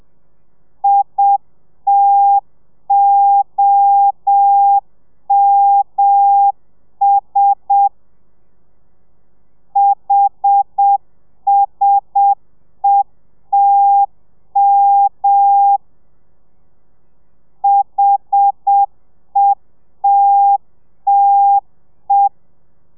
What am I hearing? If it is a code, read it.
Morse code, heard as a keyed tone, about 7 wpm, 800 Hz. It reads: ITOMS HSETM HETTE
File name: telegrafia / i2tfj / Lez1 30